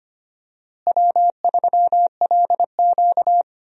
Morse code, heard as a keyed tone, about 25 wpm, 700 Hz. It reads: W3LQ
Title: USA Call Signs - 4 Characters - Repeated 2x - 25wpm